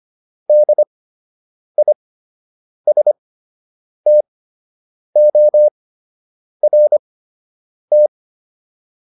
Morse code, heard as a keyed tone, about 25 wpm, spaced as for 8 wpm, 600 Hz. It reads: DISTORT